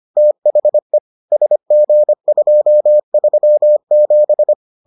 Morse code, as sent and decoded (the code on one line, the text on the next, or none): - .... / ... --. ..--- ...-- --...
THE SG237